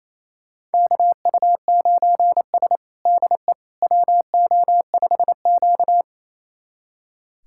Morse code, heard as a keyed tone, about 28 wpm, 700 Hz. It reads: KU9S DE WO5Q
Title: USA Callsign DE Callsign - 4 Characters - 28wpm